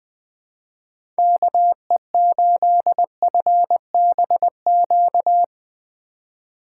Morse code, heard as a keyed tone, about 20 wpm, 700 Hz. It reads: KE8FBQ
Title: USA Call Signs - 6 Characters - 3x Character Spacing - 20wpm